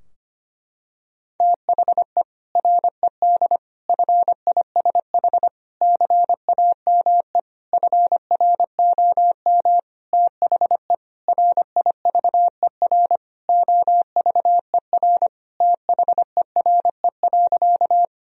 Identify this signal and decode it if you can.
Morse code, heard as a keyed tone, about 25 wpm, 700 Hz. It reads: THE RED FISH CAME FROM THE RIVER OVER THERE.
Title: Sentences from Top 1000 Words - Review Part 2 - 25wpm